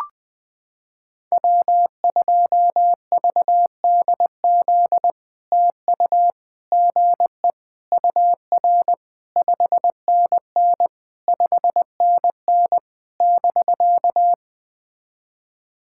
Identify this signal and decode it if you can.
Morse code, heard as a keyed tone, about 20 wpm, 700 Hz. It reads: W2VDZ TU GE UR 5NN 5NN <BK>
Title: POTA QSO Practice - 20wpm